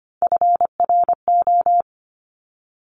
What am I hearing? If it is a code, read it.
Morse code, heard as a keyed tone, about 25 wpm, 700 Hz. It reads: FRO
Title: Top Three Letter Combinations in Words 25wpm